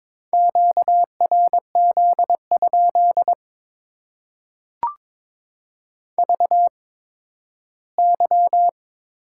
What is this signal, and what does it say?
Morse code, heard as a keyed tone, about 22 wpm, spaced as for 13 wpm, 700 Hz. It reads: QRZ? VY
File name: Top 90 QSO Elements - 8x Character Spacing - 22wpm